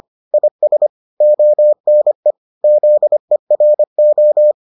Morse code, heard as a keyed tone, about 25 wpm, 600 Hz. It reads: IS ONE ZERO